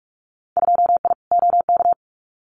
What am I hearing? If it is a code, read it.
Morse code, heard as a keyed tone, about 44 wpm, 690 Hz. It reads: 2I OK
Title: Winter Field Day Exchanges Encore 45wpm